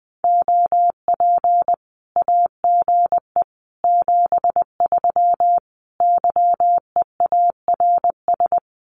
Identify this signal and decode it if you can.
Morse code, heard as a keyed tone, about 20 wpm, 700 Hz. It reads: OP AGE 73 YEARS
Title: Common QSO Phrases 20wpm